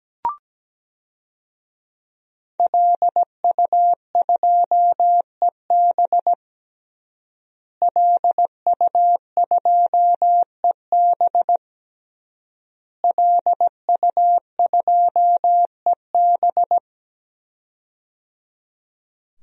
Morse code, heard as a keyed tone, about 17 wpm, 700 Hz. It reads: LU2EB LU2EB LU2EB
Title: DXCC-100 Call Signs - 5 Characters - Repeated 3x - 17wpm